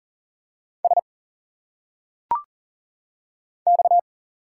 Morse code, heard as a keyed tone, about 40 wpm, 700 Hz. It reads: S X
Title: Single Letters - Phonetic Pronunciation 40wpm